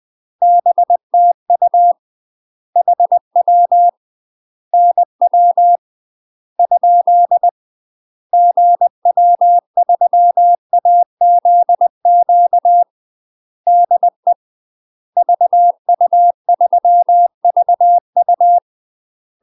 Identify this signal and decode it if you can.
Morse code, heard as a keyed tone, about 20 wpm, 700 Hz. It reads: BTU HW NW ? GW3AZQ DE VU3VU
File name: UK MASTER QSOs TRAINING/20WPM